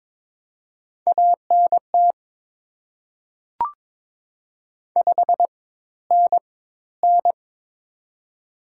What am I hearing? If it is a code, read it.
Morse code, heard as a keyed tone, about 23 wpm, spaced as for 14 wpm, 700 Hz. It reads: ANT 5NN
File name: Top 90 QSO Elements - 4x Character Spacing - 22wpm